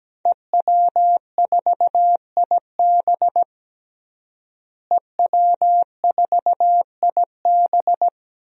Morse code, heard as a keyed tone, about 17 wpm, 700 Hz. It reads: EW4IB EW4IB
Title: DXCC-100 Call Signs - 5 Characters - Repeated 3x - 17wpm